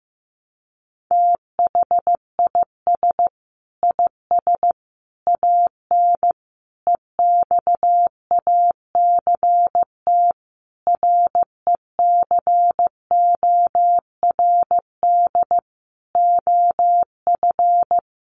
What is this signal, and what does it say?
Morse code, heard as a keyed tone, about 14 wpm, 700 Hz. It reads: THIS IS AN EXACT RECORD OF
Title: Sentences from Top 1300 Words 15wpm